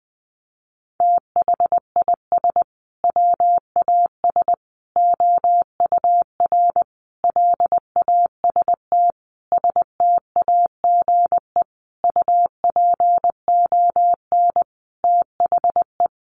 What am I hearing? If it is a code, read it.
Morse code, heard as a keyed tone, about 20 wpm, 700 Hz. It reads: THIS WAS OUR LAST STAGE UPON THE
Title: Sentences from Top 1200 Words 20wpm